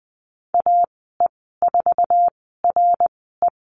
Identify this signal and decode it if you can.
Morse code, heard as a keyed tone, about 20 wpm, spaced as for 15 wpm, 700 Hz. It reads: AE4RE
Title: 055 Lesson 10g - Letter r - Call Signs 20wpm